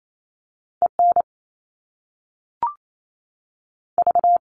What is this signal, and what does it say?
Morse code, heard as a keyed tone, about 28 wpm, 700 Hz. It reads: E N V
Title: All Cut Numbers - 2-Digit 28wpm